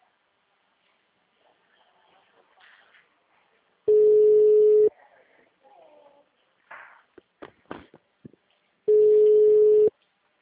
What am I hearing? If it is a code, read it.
Ringback tone (Europe, 425 Hz).
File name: out- -2003- - - .39